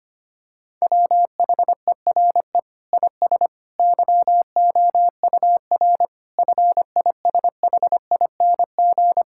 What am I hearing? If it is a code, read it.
Morse code, heard as a keyed tone, about 25 wpm, 700 Hz. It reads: WHERE IS YOUR FISHING
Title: Sentences from Top 1600 Words 25wpm